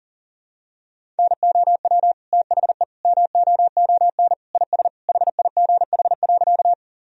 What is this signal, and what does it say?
Morse code, heard as a keyed tone, about 40 wpm, 700 Hz. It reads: NOW THE MOON IS HIGH.
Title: Sentences from Top 1000 Words - Review Part 3 - 40wpm